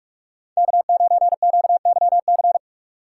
Morse code, heard as a keyed tone, about 41 wpm, 700 Hz. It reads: K9QYC